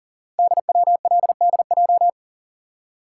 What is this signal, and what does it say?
Morse code, heard as a keyed tone, about 42 wpm, 690 Hz. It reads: DWLDJ
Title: Code Groups - 5 Letters - 40wpm